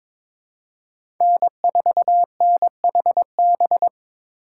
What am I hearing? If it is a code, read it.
Morse code, heard as a keyed tone, about 22 wpm, 700 Hz. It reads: N4NHB
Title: USA Call Signs - 5 Characters - 4x Character Spacing - 22wpm